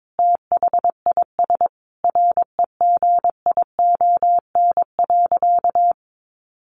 Morse code, heard as a keyed tone, about 22 wpm, 700 Hz. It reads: THIS REGION.
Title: Sentences from Top 1500 Words 22wpm